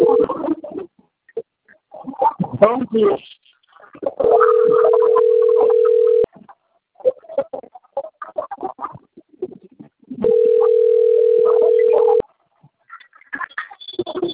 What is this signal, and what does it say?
Ringback tone (North America), 440 + 480 Hz, with speech.
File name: rg-1600-unknown- - - .9071